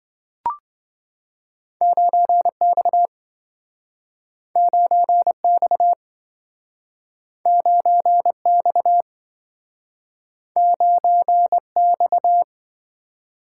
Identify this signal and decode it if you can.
Morse code, heard as a keyed tone, about 27 wpm, 700 Hz. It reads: 9X 9X 9X 9X